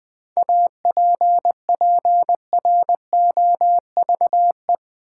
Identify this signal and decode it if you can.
Morse code, heard as a keyed tone, about 20 wpm, 700 Hz. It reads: APPROVE
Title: 7 Letter Words - First 3 Letters Given - 20wpm